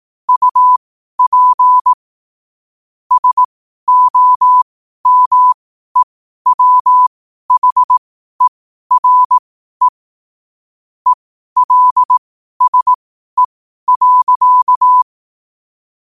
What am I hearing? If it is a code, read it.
Morse code, heard as a keyed tone, about 18 wpm, spaced as for 12 wpm, 1000 Hz. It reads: UP SOMEWHERE ELSE.
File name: qotdfiles / 12wpm